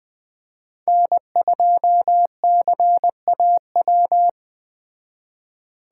Morse code, heard as a keyed tone, about 19 wpm, 700 Hz. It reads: N2CAW